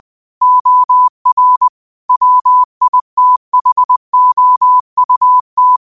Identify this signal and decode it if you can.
Morse code, heard as a keyed tone, about 20 wpm, 1000 Hz. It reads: OR WITHOUT
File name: qotdfiles / 20wpm